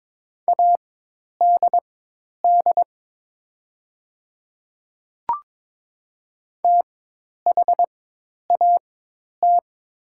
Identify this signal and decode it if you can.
Morse code, heard as a keyed tone, about 22 wpm, spaced as for 10 wpm, 690 Hz. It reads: ADD THAT
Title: 4x Character Spacing - Top 200 Words - 22wpm